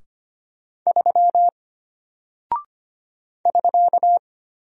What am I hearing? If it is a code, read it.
Morse code, heard as a keyed tone, about 25 wpm, 700 Hz. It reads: 3 <SK>
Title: Single Character (Letter:Number:Punctuation:Prosign) Alt 25wpm